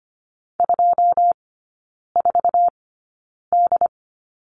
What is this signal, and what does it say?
Morse code, heard as a keyed tone, about 25 wpm, spaced as for 9 wpm, 700 Hz. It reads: 24D
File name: Lesson 17e - Letter u - Three Characters 25wpm